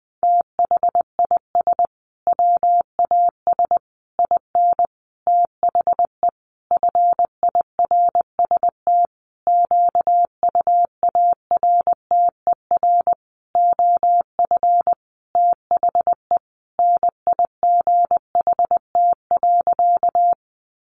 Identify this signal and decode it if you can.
Morse code, heard as a keyed tone, about 20 wpm, 700 Hz. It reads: THIS WAS IN THE FIRST QUARTER OF THE NIGHT.
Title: Sentences from Top 1300 Words 20wpm